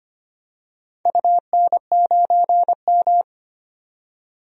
Morse code, heard as a keyed tone, about 25 wpm, 700 Hz. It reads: UN9M
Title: DXCC-100 Call Signs - 4 Characters - 25wpm